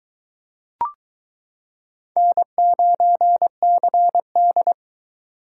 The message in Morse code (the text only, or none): N9CD